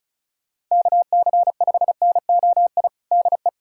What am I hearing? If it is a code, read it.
Morse code, heard as a keyed tone, about 34 wpm, 700 Hz. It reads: KC5NOI DE
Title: USA Callsign DE Callsign - 6 Characters - 35wpm